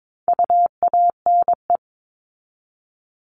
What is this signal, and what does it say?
Morse code, heard as a keyed tone, about 22 wpm, 700 Hz. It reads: UANE
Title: All Cut Numbers - 4-Digit 22wpm